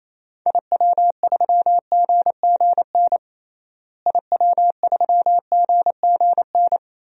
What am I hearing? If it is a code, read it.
Morse code, heard as a keyed tone, about 28 wpm, 700 Hz. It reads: IW3GGN IW3GGN
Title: DXCC-100 Call Signs - 6 Characters - Repeated 2x - 28wpm